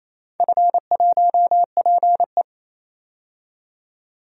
Morse code, heard as a keyed tone, about 28 wpm, 700 Hz. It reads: F1PE
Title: DXCC-100 Call Signs - 4 Characters - 28wpm